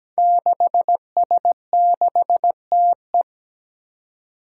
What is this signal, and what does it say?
Morse code, heard as a keyed tone, about 17 wpm, 700 Hz. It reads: 6S6TE
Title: Five Challenging Letter-Number Combinations 17wpm